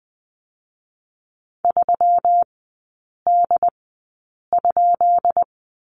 Morse code, heard as a keyed tone, about 20 wpm, spaced as for 8 wpm, 700 Hz. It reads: 3D?
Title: 191 Lesson 31e - Letter v - Three Characters 20wpm